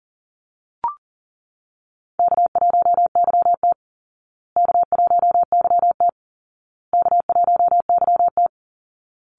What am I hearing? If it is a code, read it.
Morse code, heard as a keyed tone, about 37 wpm, 700 Hz. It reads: K1YT K1YT K1YT
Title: Easy Call Signs Repeated 3x - 40wpm